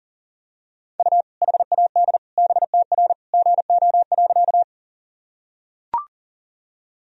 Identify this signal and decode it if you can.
Morse code, heard as a keyed tone, about 40 wpm, 700 Hz. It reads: U HAD BTR GO.
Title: Sentences from Top 400 Words and Abbreviations 40wpm